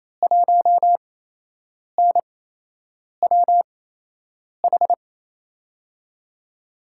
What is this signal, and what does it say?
Morse code, heard as a keyed tone, about 27 wpm, spaced as for 8 wpm, 700 Hz. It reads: 1NWH
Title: USA Call Signs - 6 Characters - 8x Character Spacing - 28wpm